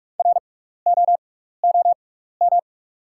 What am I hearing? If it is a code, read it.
Morse code, heard as a keyed tone, about 44 wpm, spaced as for 15 wpm, 690 Hz. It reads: ROOM
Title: 6x Character Spacing - Top 300 Words - 45wpm